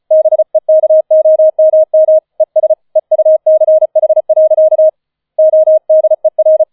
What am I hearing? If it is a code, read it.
Morse code, heard as a keyed tone, about 35 wpm, 630 Hz. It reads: BEKOMM ES EUCH. ODER
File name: 005 Fünfter Streich 35wpm